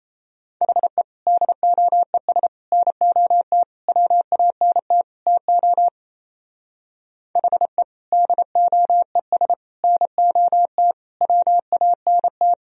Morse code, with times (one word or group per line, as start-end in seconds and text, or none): 0.6-1.0 HE
1.3-2.5 DOES
2.7-3.6 NOT
3.9-5.0 WANT
5.3-5.9 TO
7.4-7.8 HE
8.1-9.5 DOES
9.8-10.9 NOT
11.2-12.5 WANT